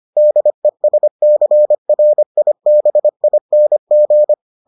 Morse code, heard as a keyed tone, about 25 wpm, 600 Hz. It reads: DESCRIBING